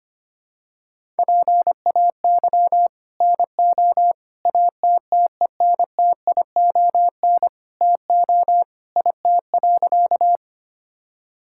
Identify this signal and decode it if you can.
Morse code, heard as a keyed tone, about 25 wpm, 700 Hz. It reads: PAY NO ATTENTION TO IT.
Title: Sentences from Top 1000 Words - Review Part 6 - 25wpm